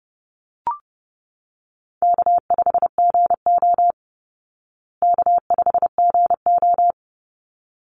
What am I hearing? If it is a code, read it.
Morse code, heard as a keyed tone, about 30 wpm, 700 Hz. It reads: K5GO K5GO
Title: Call Signs Repeated 3x - 30wpm